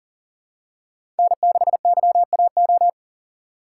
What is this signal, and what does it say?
Morse code, heard as a keyed tone, about 39 wpm, 700 Hz. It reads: N6YAO
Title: USA Call Signs - 5 Characters - 4x Character Spacing - 40wpm